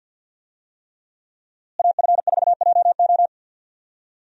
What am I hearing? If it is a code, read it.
Morse code, heard as a keyed tone, about 50 wpm, 700 Hz. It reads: AF4JO
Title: USA Call Signs - 5 Characters - Repeated 3x - 50wpm